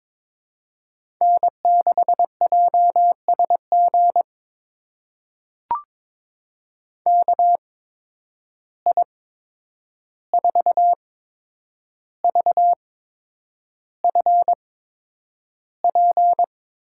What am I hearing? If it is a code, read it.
Morse code, heard as a keyed tone, about 22 wpm, spaced as for 9 wpm, 700 Hz. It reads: N6JSG KI4VFP
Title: USA Call Signs - 8x Character Spacing - 22wpm